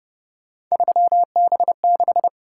Morse code, heard as a keyed tone, about 29 wpm, 700 Hz. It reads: 3B6